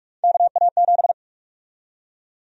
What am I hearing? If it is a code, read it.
Morse code, heard as a keyed tone, about 45 wpm, 700 Hz. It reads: KA7
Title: USA Call Sign Prefixes - 45wpm